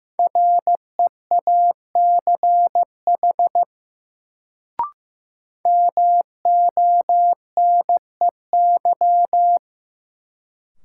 Morse code, heard as a keyed tone, about 15 wpm, 700 Hz. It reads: REACH MONEY
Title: Sentences from Top 400 Words - New Word Drill - 15wpm